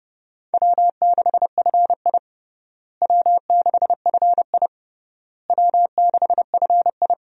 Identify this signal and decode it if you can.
Morse code, heard as a keyed tone, about 30 wpm, 700 Hz. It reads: W6FI W6FI W6FI
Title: USA Call Signs - 4 Characters - Repeated 3x - 30wpm